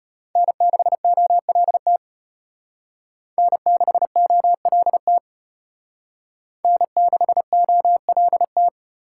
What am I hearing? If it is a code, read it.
Morse code, heard as a keyed tone, about 32 wpm, 700 Hz. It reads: N6OLT N6OLT N6OLT